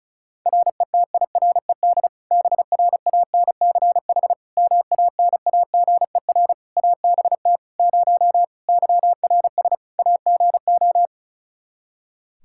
Morse code, heard as a keyed tone, about 36 wpm, 700 Hz. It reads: RETIRED BRANCH MANAGER ABT 0 YRS AGO
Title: Practice QSO - Exchange 3 - More Info - 35wpm